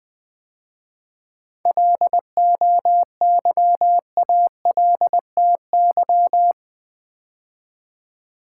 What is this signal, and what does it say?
Morse code, heard as a keyed tone, about 20 wpm, 700 Hz. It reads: LOYALTY
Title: 7 Letter Words - First 3 Letters Given - 20wpm